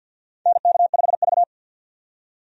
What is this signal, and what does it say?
Morse code, heard as a keyed tone, about 52 wpm, 700 Hz. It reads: NK5V
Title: USA Call Signs - 4 Characters - 50wpm